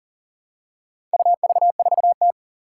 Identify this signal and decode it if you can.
Morse code, heard as a keyed tone, about 40 wpm, 700 Hz. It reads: UV4T
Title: Four Challenging Letter-Number Combinations 40wpm